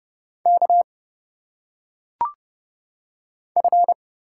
Morse code, heard as a keyed tone, about 30 wpm, 700 Hz. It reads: K F
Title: Single Letters - Phonetic Pronunciation 30wpm